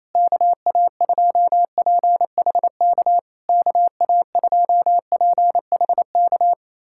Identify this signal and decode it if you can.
Morse code, heard as a keyed tone, about 28 wpm, 700 Hz. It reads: KA2PHK KA2PHK
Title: SOTA QSO Practice 28wpm